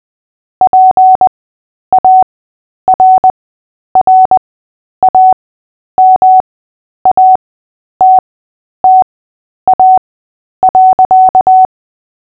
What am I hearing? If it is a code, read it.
Morse code, heard as a keyed tone, about 20 wpm, spaced as for 10 wpm, 750 Hz. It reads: PARRAMATTA.
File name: cw / fists / Short-Story-3-10wpm